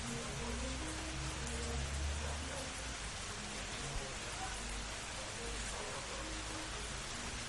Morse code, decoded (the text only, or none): none